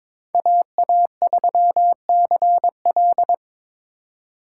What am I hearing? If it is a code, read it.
Morse code, heard as a keyed tone, about 22 wpm, 700 Hz. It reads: AA3CL
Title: USA Call Signs - 4x Character Spacing - 22wpm